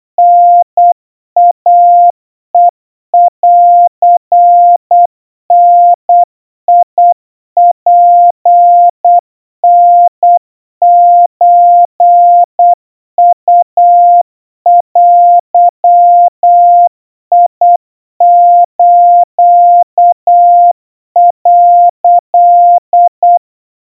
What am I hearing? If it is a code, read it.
Morse code (Japanese Wabun code), heard as a keyed tone, about 8 wpm, 700 Hz. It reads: タイヘンダツタソウデス」
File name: 30 cpm 700 Hz Crispy